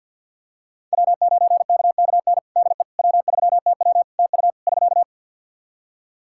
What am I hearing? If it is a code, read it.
Morse code, heard as a keyed tone, about 45 wpm, 700 Hz. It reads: W9KKN DE W3TW TU <SK>
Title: Practice QSO - Exchange 4 - Ending - 50wpm